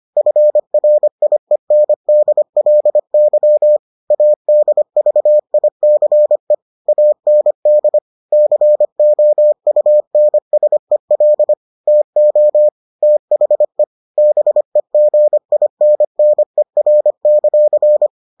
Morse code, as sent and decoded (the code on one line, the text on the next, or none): ..-. .-. .. -. -.. .-.. -.-- / .- -.. ...- .. -.-. / .- -. -.. / -.-. --- ..- -. ... .-.. / - --- / - .... / -... --. .. -. -. .-. -.-.-.
FRIENDLY ADVICE AND COUNSEL TO THE BEGINNER;